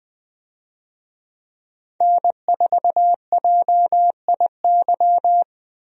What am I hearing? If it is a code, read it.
Morse code, heard as a keyed tone, about 21 wpm, 700 Hz. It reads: N4JIY
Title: USA Call Signs - 5 Characters - Repeated 3x - 20wpm